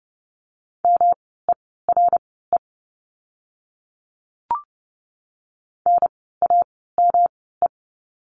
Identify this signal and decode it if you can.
Morse code, heard as a keyed tone, about 30 wpm, spaced as for 17 wpm, 700 Hz. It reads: MERE NAME
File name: course / 30:10wpm / 108 Lesson 18f - Letter m - Words 30wpm